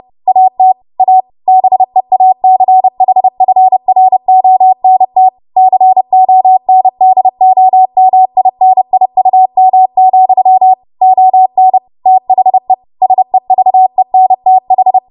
Morse code, heard as a keyed tone, about 30 wpm, 750 Hz. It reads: AT A BEACHFRONT CONDOMINIUM, ON THE SEVENTH
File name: Morse / 30WPM